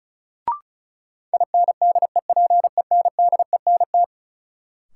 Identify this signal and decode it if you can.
Morse code, heard as a keyed tone, about 36 wpm, 700 Hz. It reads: INDEPENDENT
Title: Sentences from Top 900 Words - New Word Drill - 35wpm